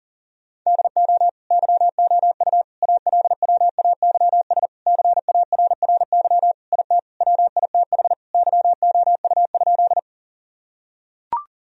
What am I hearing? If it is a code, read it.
Morse code, heard as a keyed tone, about 40 wpm, 700 Hz. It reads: DO YOU ALWAYS CARRY IT WITH YOU?
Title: Sentences from Top 1000 Words - Review Part 1 - 40wpm